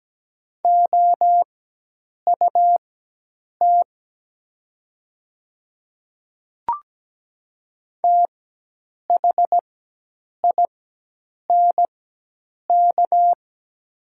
Morse code, heard as a keyed tone, about 17 wpm, spaced as for 8 wpm, 700 Hz. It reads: OUT THINK
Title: 4x Character Spacing - Top 200 Words - 17wpm